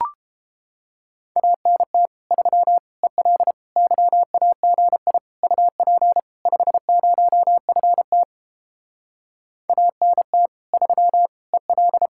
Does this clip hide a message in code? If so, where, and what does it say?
Yes: Morse code from 1.4 to 12.1 s, heard as a keyed tone, about 29 wpm, 700 Hz. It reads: ANT 3 EL YAGI UP 50FT ANT 3 EL